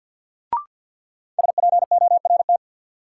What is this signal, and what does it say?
Morse code, heard as a keyed tone, about 51 wpm, 700 Hz. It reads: SPORT
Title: Sentences from Top 1500 Words - New Word Drill Alt - 50wpm